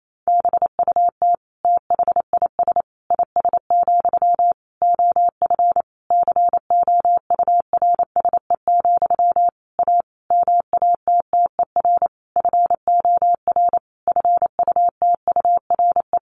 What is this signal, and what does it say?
Morse code, heard as a keyed tone, about 28 wpm, 700 Hz. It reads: BUT THIS IS, OF COURSE, A MATTER FOR FUTURE
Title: Sentences from Top 1500 Words 28wpm